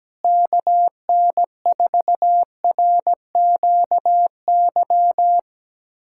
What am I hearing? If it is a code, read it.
Morse code, heard as a keyed tone, about 17 wpm, 700 Hz. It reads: KN4RQY
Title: USA Callsign DE Callsign - 6 Characters - 17wpm